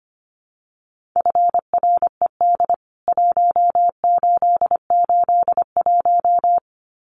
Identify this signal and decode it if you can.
Morse code, heard as a keyed tone, about 25 wpm, 700 Hz. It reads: FRED 1881
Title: CWT Exchange 25wpm